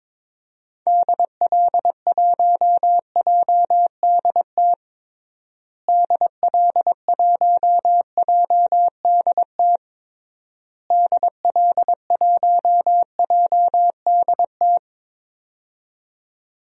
Morse code, heard as a keyed tone, about 22 wpm, 700 Hz. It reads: DL1JDT DL1JDT DL1JDT
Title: DXCC-100 Call Signs - 6 Characters - Repeated 3x - 22wpm